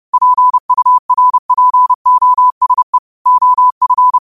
Morse code, heard as a keyed tone, about 30 wpm, 1000 Hz. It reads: PURPOSE OF